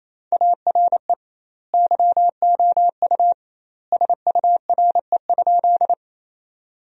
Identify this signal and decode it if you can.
Morse code, heard as a keyed tone, about 27 wpm, 700 Hz. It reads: ARE YOU SURE?
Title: 2x Word-Spacing - Sentences from Top 300 Words - 28wpm